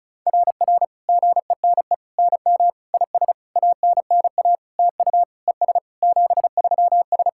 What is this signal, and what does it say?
Morse code, heard as a keyed tone, about 36 wpm, 690 Hz. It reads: RR GENE NM IS ANNA TU ES 73S
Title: SOTA QSO Practice 35wpm